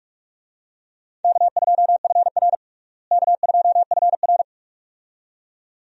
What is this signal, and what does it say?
Morse code, heard as a keyed tone, about 46 wpm, 700 Hz. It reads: K2FR K2FR
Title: USA Call Signs - 4 Characters - Repeated 2x - 45wpm